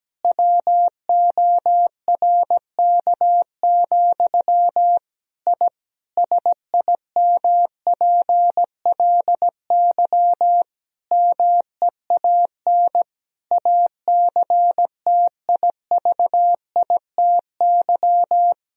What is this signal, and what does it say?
Morse code, heard as a keyed tone, about 17 wpm, 700 Hz. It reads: WORK, I SIMPLY MEAN ACTIVITY
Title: Sentences from Top 1000 Words - Review Part 4 - 17wpm